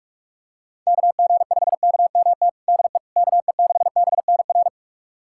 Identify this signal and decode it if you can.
Morse code, heard as a keyed tone, about 46 wpm, 700 Hz. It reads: KG5KMT DE KE6BNR
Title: USA Callsign DE Callsign - 6 Characters - 45wpm